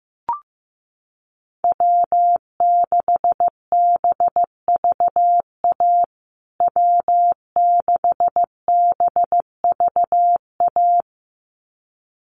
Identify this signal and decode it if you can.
Morse code, heard as a keyed tone, about 15 wpm, 700 Hz. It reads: W6BVA W6BVA
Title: USA Call Signs Repeated 2x 15wpm